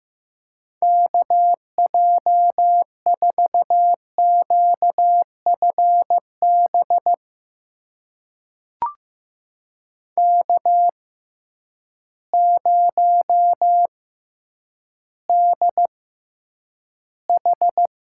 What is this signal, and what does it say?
Morse code, heard as a keyed tone, about 15 wpm, spaced as for 9 wpm, 700 Hz. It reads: KJ4QFB K0DH
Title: USA Call Signs - 6x Character Spacing - 15wpm